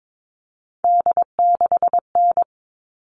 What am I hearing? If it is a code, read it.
Morse code, heard as a keyed tone, about 22 wpm, 700 Hz. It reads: D6N
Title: All Cut Numbers - 3-Digit 22wpm